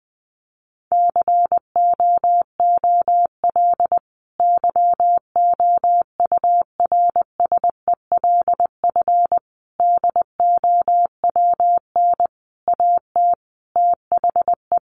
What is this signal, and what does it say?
Morse code, heard as a keyed tone, about 20 wpm, 700 Hz. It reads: COOL YOURSELF DOWN AT THE
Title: Sentences from Top 1200 Words 20wpm